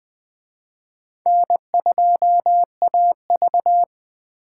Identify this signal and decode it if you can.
Morse code, heard as a keyed tone, about 20 wpm, 700 Hz. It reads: N2AV
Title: Speed Racing - USA 1x2 Call Signs 17wpm